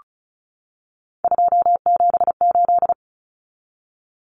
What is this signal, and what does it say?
Morse code, heard as a keyed tone, about 34 wpm, 700 Hz. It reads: 278
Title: Common Cut Numbers - 3-Digit 35wpm